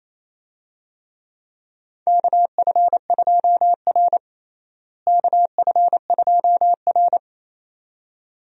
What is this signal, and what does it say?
Morse code, heard as a keyed tone, about 28 wpm, 700 Hz. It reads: KF2R KF2R
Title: USA Call Signs - 4 Characters - Repeated 2x - 28wpm